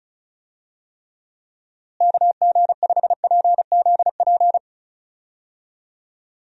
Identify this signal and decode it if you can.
Morse code, heard as a keyed tone, about 34 wpm, 700 Hz. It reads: KG5PZP
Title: USA Call Signs - 6 Characters - Repeated 3x - 35wpm